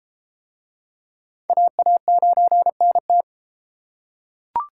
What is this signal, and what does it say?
Morse code, heard as a keyed tone, about 30 wpm, 700 Hz. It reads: AA9NT